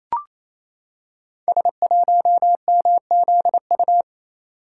Morse code, heard as a keyed tone, about 28 wpm, 700 Hz. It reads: S1MZU
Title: Code Groups - 5 Letter:Numbers - 28wpm